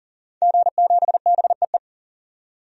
Morse code, heard as a keyed tone, about 40 wpm, 700 Hz. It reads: G7BEE
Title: DXCC-100 Call Signs - 5 Characters - 40wpm